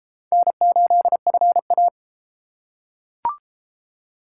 Morse code, heard as a keyed tone, about 33 wpm, 700 Hz. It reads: N8FA